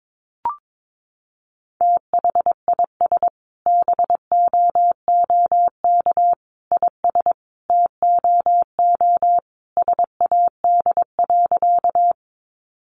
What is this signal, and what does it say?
Morse code, heard as a keyed tone, about 22 wpm, 700 Hz. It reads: THIS BOOK IS TOO SAD.
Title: Additional Easy Phrases 22wpm